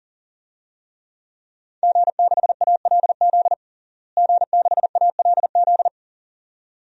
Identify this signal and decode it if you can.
Morse code, heard as a keyed tone, about 40 wpm, 700 Hz. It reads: G6ALZ G6ALZ
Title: DXCC-100 Call Signs - 5 Characters - Repeated 2x - 40wpm